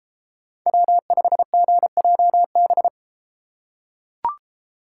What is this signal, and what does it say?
Morse code, heard as a keyed tone, about 33 wpm, 700 Hz. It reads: W5GJB